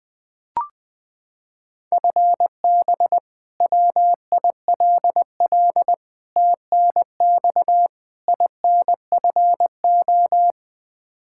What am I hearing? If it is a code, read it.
Morse code, heard as a keyed tone, about 20 wpm, 700 Hz. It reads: FB WILL TNX INFO
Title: Practice QSO - Exchange 3 - More Info - 20wpm